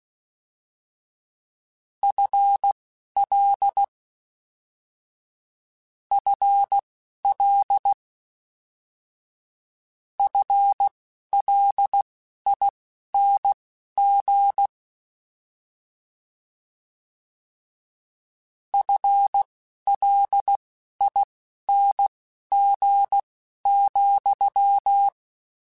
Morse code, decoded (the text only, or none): FL FL FLING FLING,